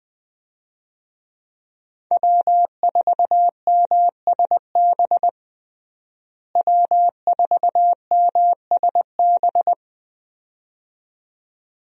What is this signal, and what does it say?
Morse code, heard as a keyed tone, about 20 wpm, 700 Hz. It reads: W4MSB W4MSB
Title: USA Call Signs - 5 Characters - Repeated 2x - 20wpm